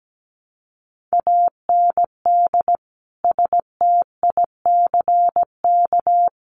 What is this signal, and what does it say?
Morse code, heard as a keyed tone, about 17 wpm, 700 Hz. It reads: AND STICK
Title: Sentences from Top 1200 Words 17wpm